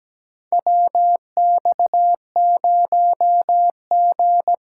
Morse code, heard as a keyed tone, about 17 wpm, 700 Hz. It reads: WX0G
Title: USA Callsign DE Callsign - 4 Characters - 17wpm